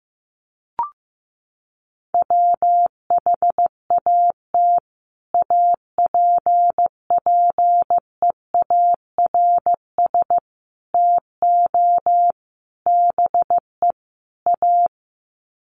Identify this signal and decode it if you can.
Morse code, heard as a keyed tone, about 15 wpm, 700 Hz. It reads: WHAT APPEARS TO BE A
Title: Sets of 5 Words Encore2 15wpm